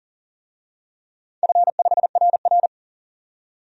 Morse code, heard as a keyed tone, about 40 wpm, 700 Hz. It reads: F5RR
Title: DXCC-100 Call Signs - 4 Characters - 40wpm